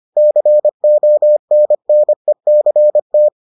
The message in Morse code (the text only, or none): CONNECT